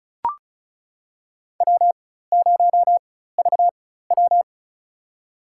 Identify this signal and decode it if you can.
Morse code, heard as a keyed tone, about 34 wpm, spaced as for 16 wpm, 700 Hz. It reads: W0VW
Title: USA Call Signs - 4x Character Spacing - 35wpm